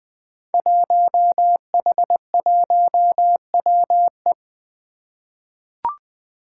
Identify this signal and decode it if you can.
Morse code, heard as a keyed tone, about 20 wpm, 700 Hz. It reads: 1H1WE
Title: Five Challenging Letter-Number Combinations 20wpm